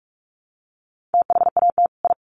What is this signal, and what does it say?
Morse code, heard as a keyed tone, about 44 wpm, 700 Hz. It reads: THAT I
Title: Sets of 2 Words 45wpm